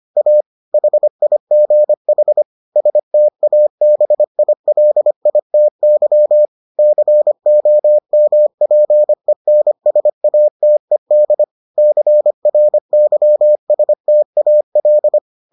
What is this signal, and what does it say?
Morse code, heard as a keyed tone, about 25 wpm, 600 Hz. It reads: A HIGH STABILITY COMPENSATED CRYSTAL